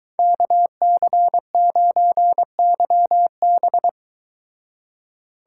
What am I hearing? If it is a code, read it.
Morse code, heard as a keyed tone, about 23 wpm, 700 Hz. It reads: KC9YB